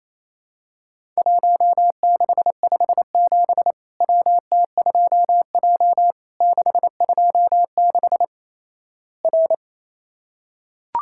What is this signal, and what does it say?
Morse code, heard as a keyed tone, about 28 wpm, 700 Hz. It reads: 1657 WT2J 626 R
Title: Worked All Europe DX Contest - QTC Exchange - 28wpm